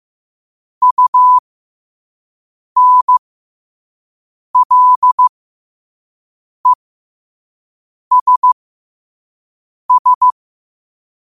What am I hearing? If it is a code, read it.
Morse code, heard as a keyed tone, about 15 wpm, spaced as for 5 wpm, 1000 Hz. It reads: UNLESS
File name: qotdfiles / 5wpm